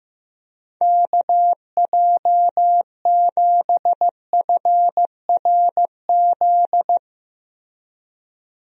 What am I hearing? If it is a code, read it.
Morse code, heard as a keyed tone, about 15 wpm, 700 Hz. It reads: KJ7FRZ
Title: USA Call Signs - 6 Characters - Repeated 3x - 15wpm